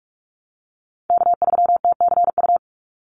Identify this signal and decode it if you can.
Morse code, heard as a keyed tone, about 44 wpm, 700 Hz. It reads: K3TCU
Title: NA SOTA Chaser Callsigns 45wpm